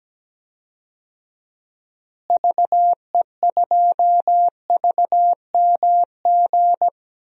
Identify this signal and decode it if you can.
Morse code, heard as a keyed tone, about 17 wpm, 700 Hz. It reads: VE2VMG
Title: DXCC-100 Call Signs - 6 Characters - Repeated 3x - 17wpm